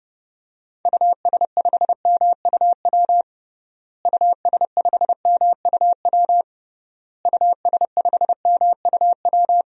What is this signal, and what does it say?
Morse code, heard as a keyed tone, about 30 wpm, 700 Hz. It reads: US5MUW US5MUW US5MUW
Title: DXCC-100 Call Signs - 6 Characters - Repeated 3x - 30wpm